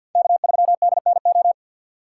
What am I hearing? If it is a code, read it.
Morse code, heard as a keyed tone, about 63 wpm, 700 Hz. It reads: K3DNO